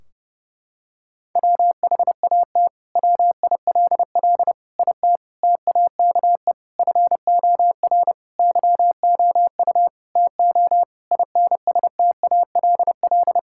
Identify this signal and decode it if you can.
Morse code, heard as a keyed tone, about 30 wpm, 700 Hz. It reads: WHAT WILL IT TAKE FOR YOU TO INSTALL
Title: Sentences from Top 1600 Words 30wpm